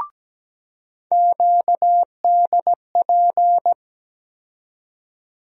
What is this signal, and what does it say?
Morse code, heard as a keyed tone, about 17 wpm, 700 Hz. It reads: QDP
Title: USA Call Sign Postfixes - 3 Letters 17wpm